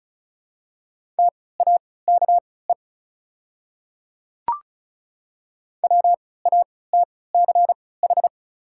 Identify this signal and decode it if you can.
Morse code, heard as a keyed tone, about 36 wpm, spaced as for 20 wpm, 690 Hz. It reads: TAKE WATCH
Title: 3x Character Spacing - Top 500 Words - 35wpm